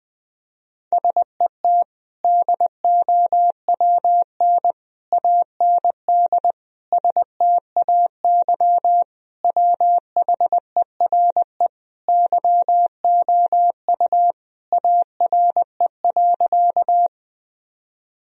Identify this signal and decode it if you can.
Morse code, heard as a keyed tone, about 20 wpm, 700 Hz. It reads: SET DOWN AND STAY WHERE YOU ARE.
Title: Sentences from Top 1000 Words - Review Part 1 - 20wpm